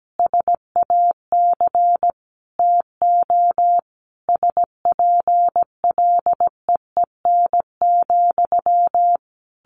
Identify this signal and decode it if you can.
Morse code, heard as a keyed tone, about 17 wpm, 700 Hz. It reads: SAC TO SPLEEN,
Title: A Sound Of Thunder 17wpm